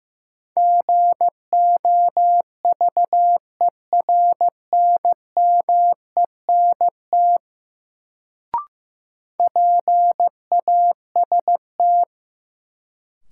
Morse code, heard as a keyed tone, about 15 wpm, 700 Hz. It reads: GOVERNMENT PAST
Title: Sentences from Top 600 Words - New Word Drill - 15wpm